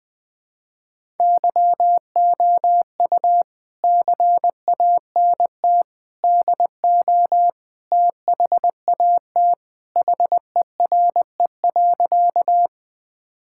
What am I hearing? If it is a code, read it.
Morse code, heard as a keyed tone, about 20 wpm, 700 Hz. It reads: YOU CANT DO THAT HERE.
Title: Sentences with 5-Word Phrases - 20wpm